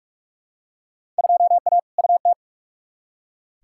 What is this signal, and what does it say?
Morse code, heard as a keyed tone, about 41 wpm, 700 Hz. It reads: 2A UT
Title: Field Day Exchanges - 45wpm